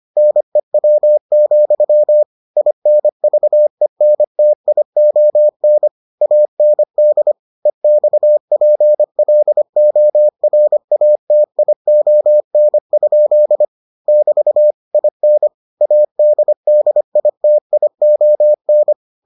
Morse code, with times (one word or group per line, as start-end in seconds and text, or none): 0.2-2.2 NEW,
2.6-5.9 INVENTION
6.2-7.3 AND
7.7-13.6 EXPLORATION?
14.1-14.7 <BT>
14.9-15.5 IN
15.8-18.9 ADDITION